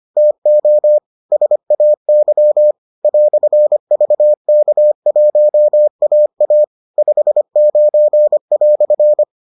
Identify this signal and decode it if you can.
Morse code, heard as a keyed tone, about 25 wpm, 600 Hz. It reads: TO SAY "VK1AA 59"